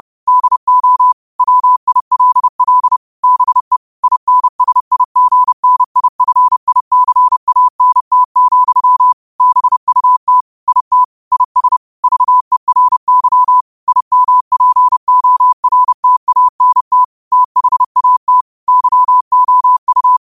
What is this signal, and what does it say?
Morse code, heard as a keyed tone, about 30 wpm, 1000 Hz. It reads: DO WILL BE INSIGNIFICANT, BUT IT IS VERY IMPORTANT THAT YOU
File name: qotdfiles / 30wpm